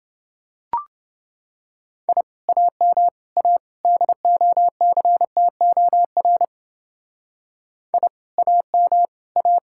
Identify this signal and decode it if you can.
Morse code, heard as a keyed tone, about 29 wpm, 700 Hz. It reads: I AM A DOCTOR I AM A